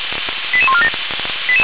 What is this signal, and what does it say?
Five-tone selective call (ZVEI), heard as quick runs of five tones, tones of about 70 ms. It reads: 88147